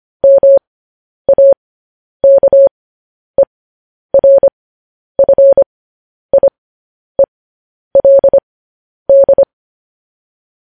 Morse code, heard as a keyed tone, about 25 wpm, spaced as for 10 wpm, 550 Hz. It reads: MAKERFIELD
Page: news-headlines-in-morse-code-at-10-wpm